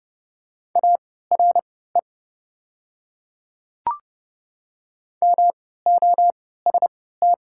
Morse code, heard as a keyed tone, about 30 wpm, spaced as for 17 wpm, 700 Hz. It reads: ARE MOST
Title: 3x Character Spacing - Top 100 Words - 30wpm